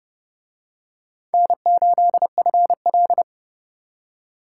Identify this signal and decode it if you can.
Morse code, heard as a keyed tone, about 29 wpm, 700 Hz. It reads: N8FL